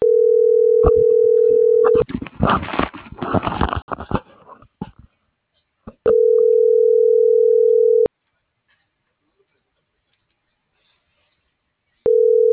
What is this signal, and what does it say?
Ringback tone (North America), 440 + 480 Hz.